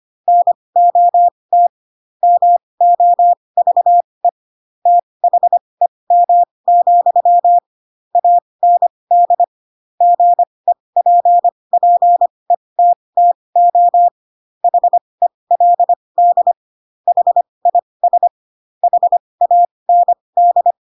Morse code, heard as a keyed tone, about 25 wpm, spaced as for 20 wpm, 700 Hz. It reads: NOT MOVE THEM, AND GEPPETTO HELD HIS HAND